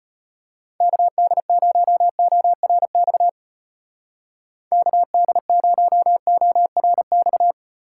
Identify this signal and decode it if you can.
Morse code, heard as a keyed tone, about 31 wpm, 700 Hz. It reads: KD0ORX KD0ORX
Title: Speed Racing - USA Call Signs 25wpm